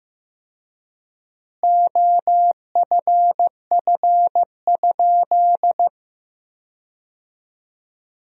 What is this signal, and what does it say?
Morse code, heard as a keyed tone, about 15 wpm, 700 Hz. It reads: OFF?
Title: 4x Word-Spacing - Sentences from Top 500 Words - 15wpm